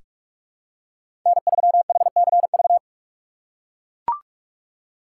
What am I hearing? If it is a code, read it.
Morse code, heard as a keyed tone, about 41 wpm, 700 Hz. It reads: N3HCV
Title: USA Call Signs - 5 Characters - 4x Character Spacing - 45wpm